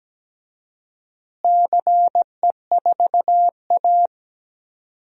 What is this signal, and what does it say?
Morse code, heard as a keyed tone, about 17 wpm, 700 Hz. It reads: CE4A
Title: DXCC-100 Call Signs - 4 Characters - Repeated 2x - 17wpm